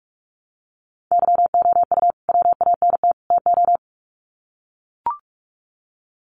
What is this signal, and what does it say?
Morse code, heard as a keyed tone, about 44 wpm, 700 Hz. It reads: YOU WANT TO
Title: Sets of 3 Words 45wpm